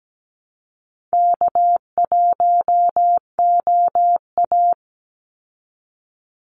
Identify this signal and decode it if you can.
Morse code, heard as a keyed tone, about 17 wpm, 700 Hz. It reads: K1OA
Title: Call Signs Repeated 3x - 17wpm